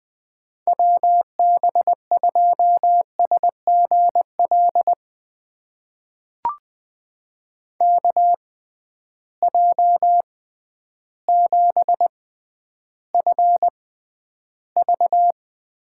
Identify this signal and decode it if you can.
Morse code, heard as a keyed tone, about 20 wpm, spaced as for 11 wpm, 700 Hz. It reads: WB2SGL KJ7FV
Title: USA Call Signs - 6 Characters - 6x Character Spacing - 20wpm